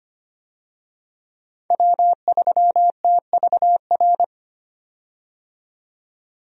Morse code, heard as a keyed tone, about 25 wpm, 700 Hz. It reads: W3TVR
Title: USA Call Signs - 5 Characters - 25wpm